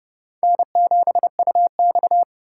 Morse code, heard as a keyed tone, about 30 wpm, 700 Hz. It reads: N7UX
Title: USA Call Signs - 4x Character Spacing - 30wpm